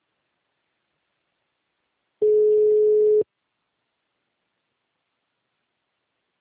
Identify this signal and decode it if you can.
Ringback tone (Europe, 425 Hz).
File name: out- -2004- - - .0